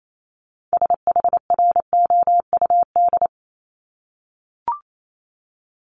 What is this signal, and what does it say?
Morse code, heard as a keyed tone, about 28 wpm, 700 Hz. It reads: SHROUD
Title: Multi-letter Phonemes 28wpm